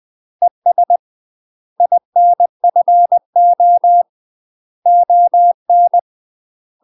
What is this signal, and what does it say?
Morse code, heard as a keyed tone, about 20 wpm, 700 Hz. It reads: ES INFO ON
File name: UK MASTER QSOs TRAINING / 20WPM / UK MASTER QSOs 0051